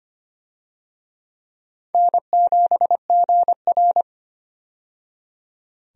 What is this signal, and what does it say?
Morse code, heard as a keyed tone, about 25 wpm, 700 Hz. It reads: N7GR
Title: USA Call Signs - 4 Characters - 25wpm